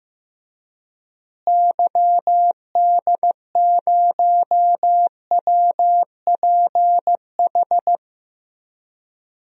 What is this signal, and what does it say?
Morse code, heard as a keyed tone, about 15 wpm, 700 Hz. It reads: YD0WPH
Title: DXCC-100 Call Signs - 6 Characters - Repeated 3x - 15wpm